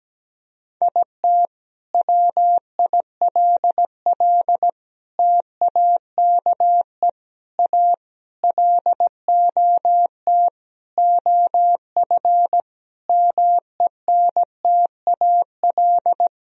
Morse code, heard as a keyed tone, about 17 wpm, 700 Hz. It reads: IT WILL TAKE A LOT OF MENTAL
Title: Sentences from Top 1000 Words - Review Part 4 - 17wpm